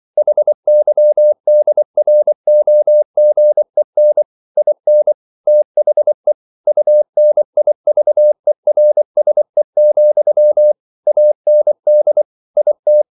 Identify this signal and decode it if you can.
Morse code, heard as a keyed tone, about 24 wpm, 600 Hz. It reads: HYDROGEN IN THE UNIVERSE, AND IT
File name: CW / Practice / EnglishQuotations0224